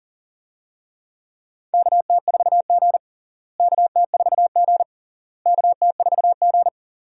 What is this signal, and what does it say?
Morse code, heard as a keyed tone, about 37 wpm, 700 Hz. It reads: KT4G KT4G KT4G
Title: USA Call Signs - 4 Characters - Repeated 3x - 40wpm